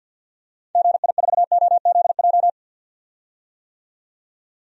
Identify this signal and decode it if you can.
Morse code, heard as a keyed tone, about 52 wpm, 700 Hz. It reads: GI4OZJ
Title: DXCC-100 Call Signs - 6 Characters - 50wpm